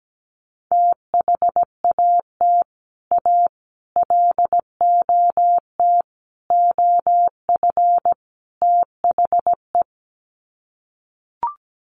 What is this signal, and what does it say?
Morse code, heard as a keyed tone, about 17 wpm, 700 Hz. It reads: THAT A LOT OF THE
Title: Sets of 5 Words Encore2 17wpm